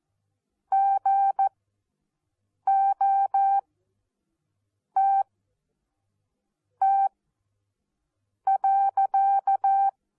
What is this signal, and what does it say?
Morse code, heard as a keyed tone, about 14 wpm, spaced as for 5 wpm, 780 Hz. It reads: GOTT.
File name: Onlinelehrgang / morsen / DJ4UF Morsen14